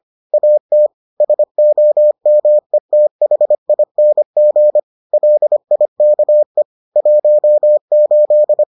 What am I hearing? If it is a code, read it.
Morse code, heard as a keyed tone, about 25 wpm, 600 Hz. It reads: AT SOMETHING LIKE 18